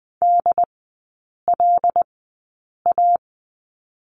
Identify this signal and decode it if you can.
Morse code, heard as a keyed tone, about 20 wpm, spaced as for 8 wpm, 690 Hz. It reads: DLA
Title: 093 Lesson 16e - Letter c - Three Characters 20wpm